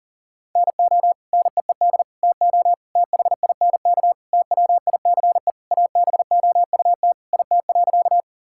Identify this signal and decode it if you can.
Morse code, heard as a keyed tone, about 40 wpm, 700 Hz. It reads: NO NEED TO THINK TWICE ABOUT IT.
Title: Sentences from Top 1600 Words 40wpm